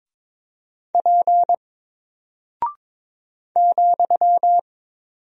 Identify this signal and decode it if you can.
Morse code, heard as a keyed tone, about 22 wpm, 700 Hz. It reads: P ,
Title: Single Character (Letter:Number:Punctuation:Prosign) Alt 22wpm